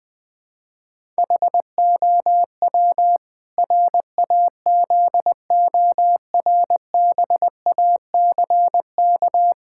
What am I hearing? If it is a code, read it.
Morse code, heard as a keyed tone, about 20 wpm, 700 Hz. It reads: HOW RAZORBACK
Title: Pangram Challenge v2 - 20wpm